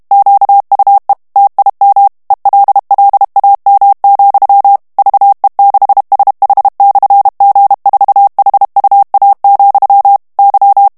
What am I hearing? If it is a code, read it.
Morse code, heard as a keyed tone, about 32 wpm, 800 Hz. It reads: QUE TIM ELLAM, VE6SH/G4HUA, Y
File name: morse mp3 / ESPAGNOL TEXTE EN CLAIR 1 VITESSE 30 MOTS MINUTES